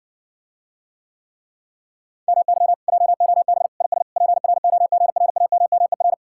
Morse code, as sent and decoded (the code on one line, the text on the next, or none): -- -.-- / .--- --- -... / .. ... / .--. .-. --- --. .-. .- -- -- . .-.
MY JOB IS PROGRAMMER